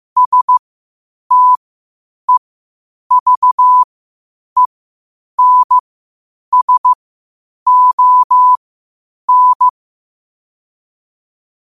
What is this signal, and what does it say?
Morse code, heard as a keyed tone, about 15 wpm, spaced as for 8 wpm, 1000 Hz. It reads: STEVENSON
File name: qotdfiles / 8wpm